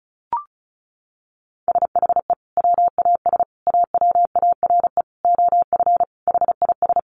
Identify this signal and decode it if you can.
Morse code, heard as a keyed tone, about 34 wpm, 700 Hz. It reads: SHE WAS AWARE OF HIS
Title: Sentences from Top 1200 Words 35wpm